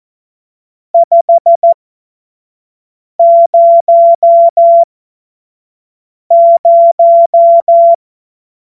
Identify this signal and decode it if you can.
Morse code, heard as a keyed tone, about 14 wpm, spaced as for 5 wpm, 680 Hz. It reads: 500
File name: Morsekurs / Witze / Witz 06. April 2021, 25 F70